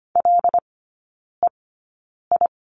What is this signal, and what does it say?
Morse code, heard as a keyed tone, about 25 wpm, spaced as for 9 wpm, 700 Hz. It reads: LEI
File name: Lesson 14e - Number 2 - Three Characters 25wpm